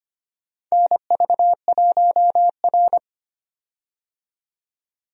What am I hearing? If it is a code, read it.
Morse code, heard as a keyed tone, about 25 wpm, 700 Hz. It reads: NV1R
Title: USA Call Signs - 4 Characters - 25wpm